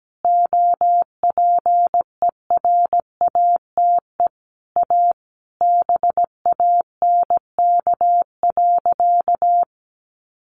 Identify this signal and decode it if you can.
Morse code, heard as a keyed tone, about 17 wpm, 700 Hz. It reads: OPERATE A BANK.
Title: Sentences from Top 1500 Words 17wpm